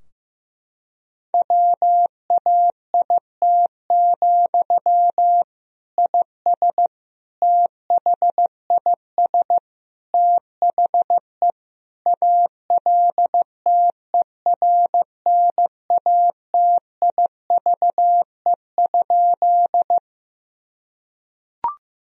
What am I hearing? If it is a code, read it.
Morse code, heard as a keyed tone, about 15 wpm, 700 Hz. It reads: WAIT, IS THIS THE ALTERNATIVE?
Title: Sentences from Top 1000 Words - Review Part 4 - 15wpm